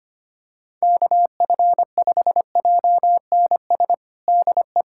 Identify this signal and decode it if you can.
Morse code, heard as a keyed tone, about 25 wpm, 700 Hz. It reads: KF5JNS DE
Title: USA Callsign DE Callsign - 6 Characters - 25wpm